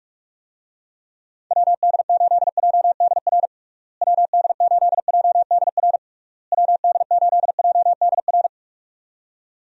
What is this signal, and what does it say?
Morse code, heard as a keyed tone, about 46 wpm, 700 Hz. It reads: WD8JDR WD8JDR WD8JDR
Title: USA Call Signs - 6 Characters - Repeated 3x - 45wpm